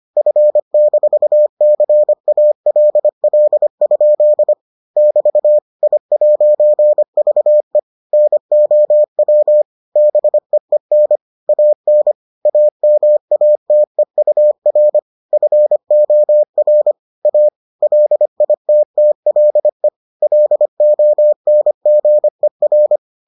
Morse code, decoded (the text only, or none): F-CALL? <BT> I'VE NOW BEEN AN AMATEUR FOR A LITTLE LONGER